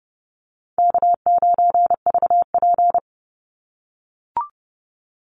Morse code, heard as a keyed tone, about 30 wpm, 700 Hz. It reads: K9VP
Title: Easy Call Signs - 30wpm